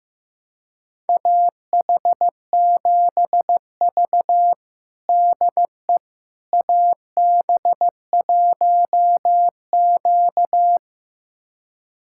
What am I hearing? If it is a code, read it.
Morse code, heard as a keyed tone, about 16 wpm, 700 Hz. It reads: AH7V DE AB1Q
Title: USA Callsign DE Callsign - 4 Characters - 15wpm